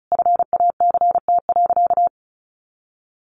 Morse code, heard as a keyed tone, about 34 wpm, 700 Hz. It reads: FACT.
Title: Sentences from Top 1300 Words 35wpm